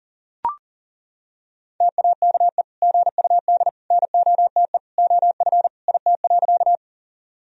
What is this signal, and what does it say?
Morse code, heard as a keyed tone, about 37 wpm, 700 Hz. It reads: TAKE GUD NOTE OF IT.
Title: Sentences from Top 500 Words and Abbreviations 40wpm